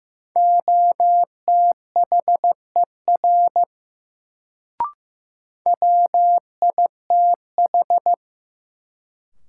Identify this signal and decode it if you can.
Morse code, heard as a keyed tone, about 15 wpm, 700 Hz. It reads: OTHER WITH
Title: Sets of 2 Words - New Word Drill - 15wpm